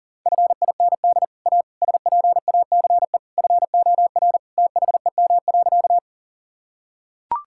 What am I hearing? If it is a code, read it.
Morse code, heard as a keyed tone, about 37 wpm, 700 Hz. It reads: FIND A SPACE FOR THEM.
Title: Sentences from Top 1000 Words - Review Part 5 - 40wpm